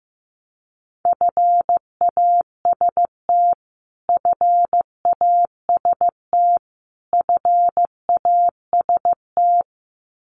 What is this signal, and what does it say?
Morse code, heard as a keyed tone, about 15 wpm, 700 Hz. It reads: FAST FAST FAST
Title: Top 500 Words Repeated 15wpm